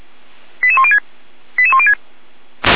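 Five-tone selective call (ZVEI), heard as quick runs of five tones, tones of about 70 ms. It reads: 88106 88106
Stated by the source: Five-tone ZVEI selective call received by SDR, from audio server of feuerwehr-orlamuende.org fire brigade.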